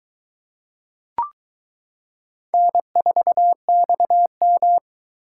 Morse code, heard as a keyed tone, about 22 wpm, 700 Hz. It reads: N4XM